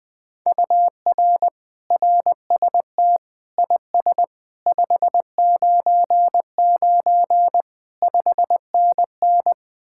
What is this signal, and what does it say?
Morse code, heard as a keyed tone, about 20 wpm, 700 Hz. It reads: UR RST IS 599 5NN
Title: Speed-Racing Common QSO Phrases - 17wpm